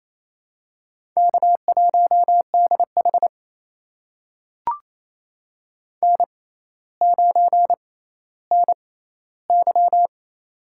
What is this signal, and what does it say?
Morse code, heard as a keyed tone, about 28 wpm, spaced as for 14 wpm, 700 Hz. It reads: K1DH N9NY
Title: USA Call Signs - 4 Characters - 6x Character Spacing - 28wpm